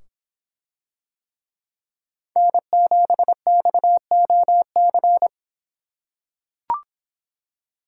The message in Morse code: N7XOC

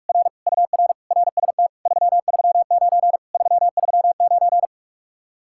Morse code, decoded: R UR RST 339 339

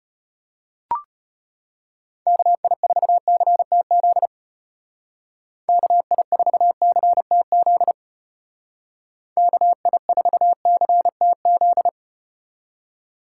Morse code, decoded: KI4CTZ KI4CTZ KI4CTZ

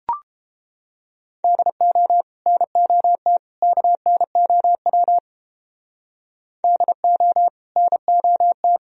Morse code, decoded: DO NOT KNOW DO NOT

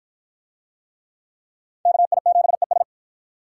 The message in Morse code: KI7ES

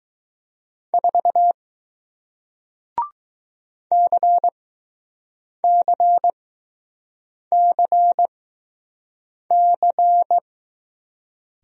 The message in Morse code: 4 CCCC

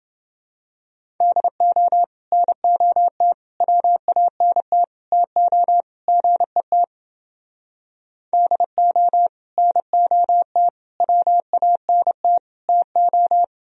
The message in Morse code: DO NOT WANT TO GET DO NOT WANT TO

